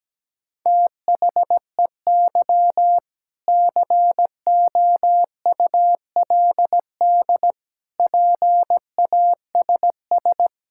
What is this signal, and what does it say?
Morse code, heard as a keyed tone, about 17 wpm, 700 Hz. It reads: THEY COULD PASS
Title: Sentences from Top 1000 Words - Review Part 6 - 17wpm